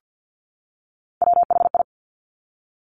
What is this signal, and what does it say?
Morse code, heard as a keyed tone, about 50 wpm, 700 Hz. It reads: WHI
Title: Top Three Letter Combinations in Words 50wpm